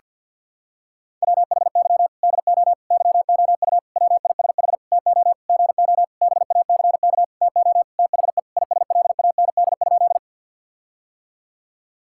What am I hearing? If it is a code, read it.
Morse code, heard as a keyed tone, about 50 wpm, 700 Hz. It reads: WHY DO YOU WISH TO GO BACK TO THE ISLAND?
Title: Sentences from Top 1600 Words 50wpm